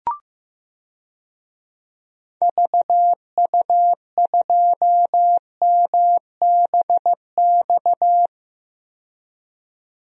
Morse code, heard as a keyed tone, about 15 wpm, 700 Hz. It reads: VU2MBX
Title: DXCC-100 Call Signs - 6 Characters - 15wpm